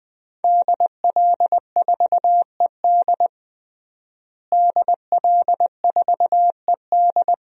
Morse code, heard as a keyed tone, about 20 wpm, 700 Hz. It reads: DL4ED DL4ED
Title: DXCC-100 Call Signs - 5 Characters - Repeated 2x - 20wpm